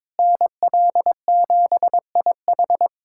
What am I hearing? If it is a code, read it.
Morse code, heard as a keyed tone, about 22 wpm, 700 Hz. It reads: NL7IH
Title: USA Callsign DE Callsign 22wpm